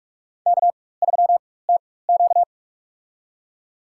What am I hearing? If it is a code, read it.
Morse code, heard as a keyed tone, about 46 wpm, spaced as for 21 wpm, 700 Hz. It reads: K3TQ
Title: USA Call Signs - 4 Characters - 4x Character Spacing - 45wpm